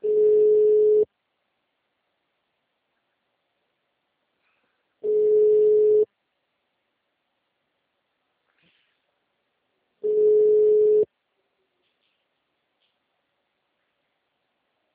Ringback tone (Europe, 425 Hz).
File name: out- -2003- - - .19209